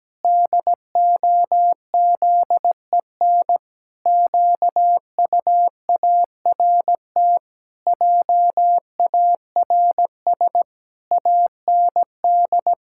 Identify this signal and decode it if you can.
Morse code, heard as a keyed tone, about 17 wpm, 700 Hz. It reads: DOZEN QUART JARS AND
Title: Pangram Challenge v2 - 17wpm